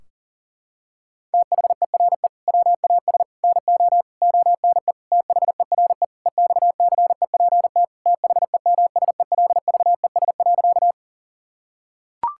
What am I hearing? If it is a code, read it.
Morse code, heard as a keyed tone, about 40 wpm, 700 Hz. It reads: THERE WAS NO ONE THERE EXCEPT THEMSELVES.
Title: Sentences from Top 1600 Words 40wpm